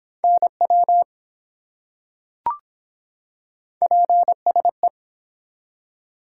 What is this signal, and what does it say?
Morse code, heard as a keyed tone, about 24 wpm, 700 Hz. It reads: NW PSE